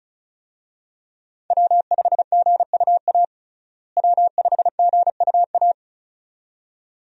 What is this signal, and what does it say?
Morse code, heard as a keyed tone, about 36 wpm, 700 Hz. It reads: W5GUA W5GUA
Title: USA Call Signs - 5 Characters - Repeated 2x - 35wpm